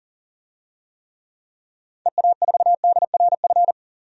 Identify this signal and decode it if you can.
Morse code, heard as a keyed tone, about 40 wpm, 700 Hz. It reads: EA4DRF
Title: DXCC-100 Call Signs - 6 Characters - 40wpm